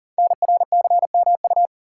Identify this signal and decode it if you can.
Morse code, heard as a keyed tone, about 42 wpm, 690 Hz. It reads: NRCMU
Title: Code Groups - 5 Letters - 40wpm